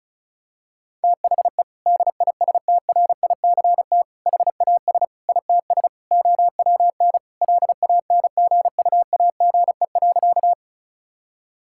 Morse code, heard as a keyed tone, about 36 wpm, 700 Hz. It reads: THE DISTRICT HAS ITS OWN LANGUAGE.
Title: Sentences from Top 1600 Words 35wpm